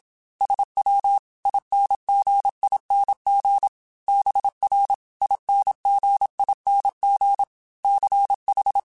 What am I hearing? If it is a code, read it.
Morse code, heard as a keyed tone, about 27 wpm, 800 Hz. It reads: SW INGING BR INGING CH